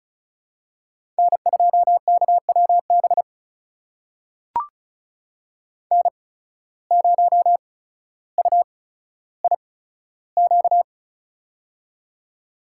Morse code, heard as a keyed tone, about 36 wpm, spaced as for 15 wpm, 700 Hz. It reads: N2KWB N0UIQ
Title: USA Call Signs - 5 Characters - 8x Character Spacing - 35wpm